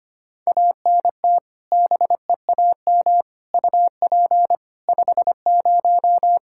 Morse code, heard as a keyed tone, about 25 wpm, 700 Hz. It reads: ANT BEAM UP 50
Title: Practice QSO - Exchange 2 - Info - 25wpm